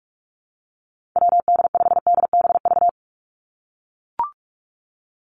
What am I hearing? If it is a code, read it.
Morse code, heard as a keyed tone, about 44 wpm, 700 Hz. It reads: WD5DBV
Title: Call Signs - 45wpm